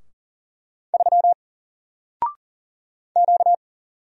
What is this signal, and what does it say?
Morse code, heard as a keyed tone, about 39 wpm, 700 Hz. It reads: 3 Q
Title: Single Letter-Number Alt 40wpm